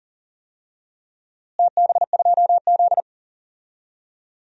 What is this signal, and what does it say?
Morse code, heard as a keyed tone, about 40 wpm, 690 Hz. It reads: TB2Z
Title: DXCC-100 Call Signs - 4 Characters - Repeated 2x - 40wpm